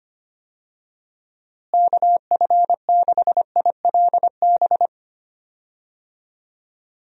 Morse code, heard as a keyed tone, about 25 wpm, 700 Hz. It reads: KF6ILB
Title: USA Call Signs - 6 Characters - Repeated 2x - 25wpm